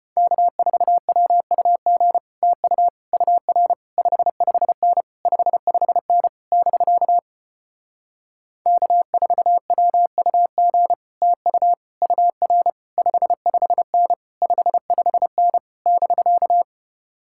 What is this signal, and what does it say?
Morse code, heard as a keyed tone, about 33 wpm, 700 Hz. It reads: K4WUG TU UR 55N 55N <BK> K4WUG TU UR 55N 55N <BK>